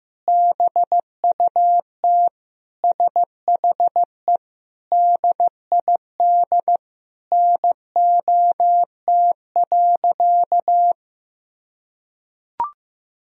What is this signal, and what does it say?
Morse code, heard as a keyed tone, about 15 wpm, 700 Hz. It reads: BUT SHE DID NOT.
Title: Sentences from Top 1000 Words - Review Part 1 - 15wpm